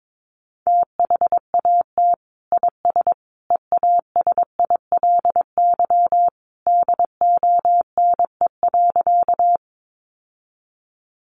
Morse code, heard as a keyed tone, about 22 wpm, 700 Hz. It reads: THAT IS EASILY DONE.
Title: Sentences from Top 900 Words 22wpm